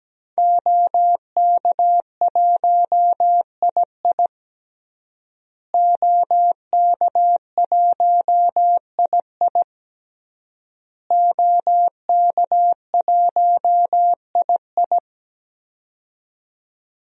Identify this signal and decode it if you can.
Morse code, heard as a keyed tone, about 17 wpm, 700 Hz. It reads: OK1II OK1II OK1II
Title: DXCC-100 Call Signs - 5 Characters - Repeated 3x - 17wpm